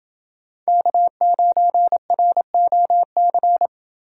Morse code, heard as a keyed tone, about 29 wpm, 700 Hz. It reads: K9ROC